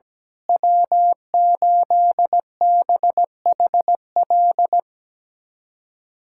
Morse code, heard as a keyed tone, about 17 wpm, 700 Hz. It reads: W8BHL
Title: USA Callsign DE Callsign - 5 Characters - 17wpm